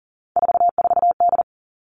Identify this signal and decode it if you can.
Morse code, heard as a keyed tone, about 40 wpm, 700 Hz. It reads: 44D
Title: All Cut Numbers - 3-Digit 40wpm